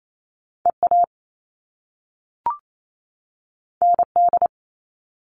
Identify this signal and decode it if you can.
Morse code, heard as a keyed tone, about 28 wpm, 700 Hz. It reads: EA ND
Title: All Cut Numbers - 2-Digit 28wpm